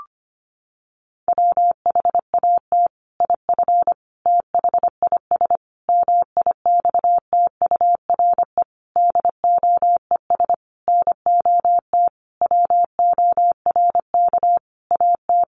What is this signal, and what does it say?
Morse code, heard as a keyed tone, about 25 wpm, 700 Hz. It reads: WHAT IF THIS MIXTURE DOES NOT WORK AT
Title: Sentences from Top 1500 Words 25wpm